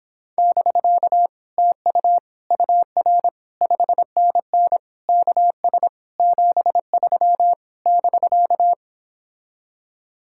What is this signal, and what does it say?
Morse code, heard as a keyed tone, about 26 wpm, 700 Hz. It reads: <BK> TU UR 5NN KS 73 <BK>